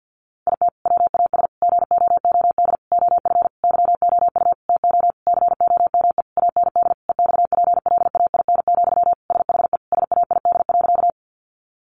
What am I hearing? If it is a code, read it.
Morse code, heard as a keyed tone, about 50 wpm, 700 Hz. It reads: IT WAS GOOD OF YOU TO COME AND EXPLAIN, SHE SAID.
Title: Sentences from Top 1100 Words 50wpm